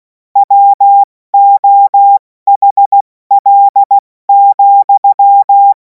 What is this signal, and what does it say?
Morse code, heard as a keyed tone, about 16 wpm, 800 Hz. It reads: WOHL,
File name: Morsekurs / Witze / Witz 24. Februar 2026 70 F80